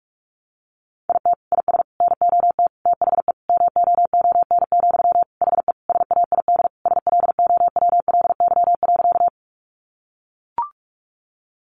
Morse code, heard as a keyed tone, about 44 wpm, 700 Hz. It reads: IT IS NOT THE MOON, HE SAID SLOWLY.
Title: Sentences from Top 1100 Words 45wpm